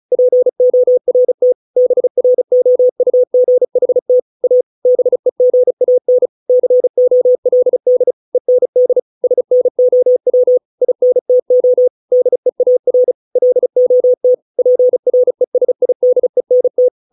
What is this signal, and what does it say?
Morse code, heard as a keyed tone, about 36 wpm, 500 Hz. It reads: PORT BROUGHT A BEGAN COLD END SNOW INTO DEAR LOT PRESIDENT